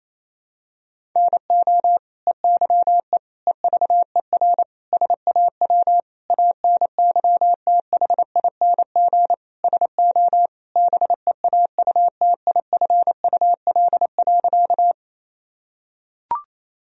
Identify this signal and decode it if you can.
Morse code, heard as a keyed tone, about 28 wpm, 700 Hz. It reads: NO EYE EVER SAW ANYTHING SO BEAUTIFUL.
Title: Sentences from Top 1000 Words - Review Part 1 - 28wpm